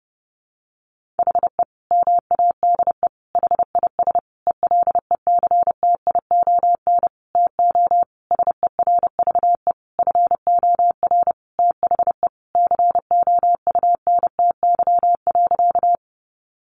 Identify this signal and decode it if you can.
Morse code, heard as a keyed tone, about 30 wpm, 700 Hz. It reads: HE MADE HIS ELECTION TO SERVE FOR THE COUNTY.
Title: Sentences from Top 1500 Words 30wpm